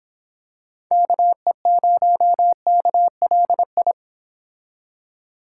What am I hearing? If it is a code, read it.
Morse code, heard as a keyed tone, about 26 wpm, 700 Hz. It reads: KE0KLI